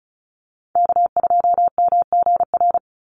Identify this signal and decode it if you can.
Morse code, heard as a keyed tone, about 34 wpm, 700 Hz. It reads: K2MGR
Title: Call Signs - 35wpm